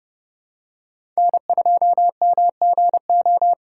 Morse code, heard as a keyed tone, about 30 wpm, 700 Hz. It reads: N2MGO
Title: USA Call Signs - 8x Character Spacing - 30wpm